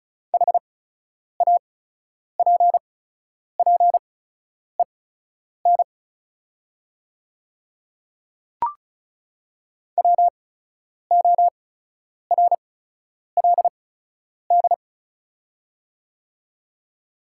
Morse code, heard as a keyed tone, about 34 wpm, spaced as for 10 wpm, 690 Hz. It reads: HAPPEN WORLD
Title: 8x Character Spacing - Top 500 Words - 35wpm